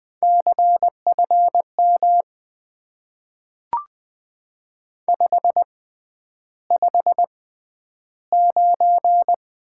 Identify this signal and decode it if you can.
Morse code, heard as a keyed tone, about 20 wpm, spaced as for 10 wpm, 700 Hz. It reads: CFM 559
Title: Top 90 QSO Elements - 6x Character Spacing - 20wpm